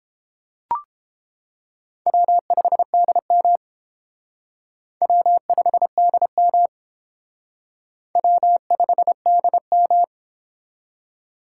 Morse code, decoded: W5DM W5DM W5DM

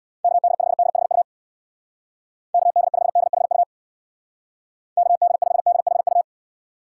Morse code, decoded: KD5DHU KD5DHU KD5DHU

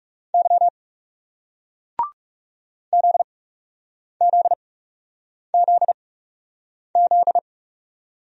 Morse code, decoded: Y ZZZZ

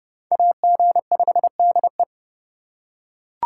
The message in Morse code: AG5DE

